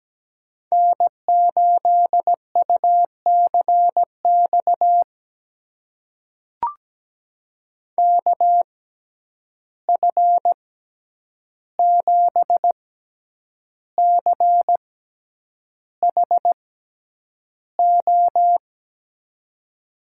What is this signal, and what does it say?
Morse code, heard as a keyed tone, about 17 wpm, spaced as for 8 wpm, 700 Hz. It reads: N8UCX KF7CHO
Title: USA Call Signs - 6x Character Spacing - 17wpm